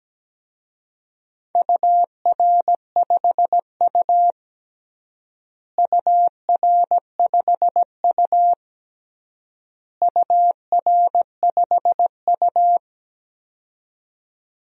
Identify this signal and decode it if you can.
Morse code, heard as a keyed tone, about 17 wpm, 700 Hz. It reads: UR5U UR5U UR5U
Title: DXCC-100 Call Signs - 4 Characters - Repeated 3x - 17wpm